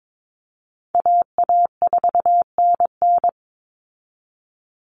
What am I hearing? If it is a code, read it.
Morse code, heard as a keyed tone, about 22 wpm, 700 Hz. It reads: AA4NN
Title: Call Signs Repeated 3x - 22wpm